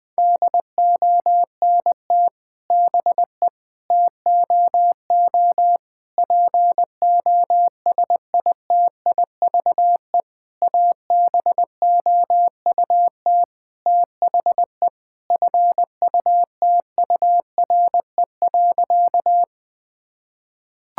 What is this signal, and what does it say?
Morse code, heard as a keyed tone, about 20 wpm, 700 Hz. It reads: DONT BE TOO POSITIVE ABOUT THE FUTURE.
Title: Sentences from Top 1000 Words - Review Part 5 - 20wpm